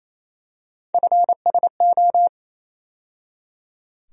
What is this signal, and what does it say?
Morse code, heard as a keyed tone, about 28 wpm, 700 Hz. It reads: FSO
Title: USA Call Sign Postfixes - 3 Letters 28wpm